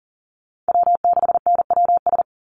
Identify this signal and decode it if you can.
Morse code, heard as a keyed tone, about 37 wpm, 700 Hz. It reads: W6NWS
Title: Call Signs - 40wpm